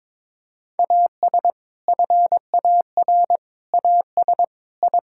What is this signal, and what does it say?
Morse code, heard as a keyed tone, about 22 wpm, 700 Hz. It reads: AS FAR AS I
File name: practice / 22wpm / Build A Phrase - 5-Word Phrases - 22wpm